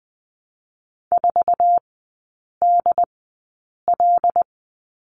Morse code, heard as a keyed tone, about 20 wpm, spaced as for 8 wpm, 700 Hz. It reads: 4DL
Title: 074 Lesson 13e - Letter L - Three Characters 20wpm